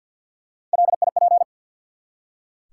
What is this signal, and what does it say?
Morse code, heard as a keyed tone, about 47 wpm, 700 Hz. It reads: LIP
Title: Three Letter Words - 50wpm